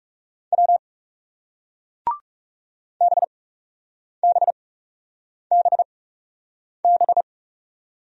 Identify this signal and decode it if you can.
Morse code, heard as a keyed tone, about 40 wpm, spaced as for 8 wpm, 700 Hz. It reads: W BBBB